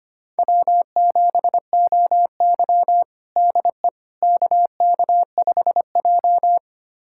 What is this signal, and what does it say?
Morse code, heard as a keyed tone, about 25 wpm, 700 Hz. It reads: W7OY DE KK5J
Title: USA Callsign DE Callsign - 4 Characters - 25wpm